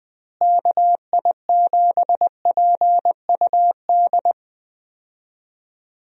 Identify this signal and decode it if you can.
Morse code, heard as a keyed tone, about 21 wpm, 700 Hz. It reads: KI7PUD
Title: USA Call Signs - 6 Characters - Repeated 2x - 20wpm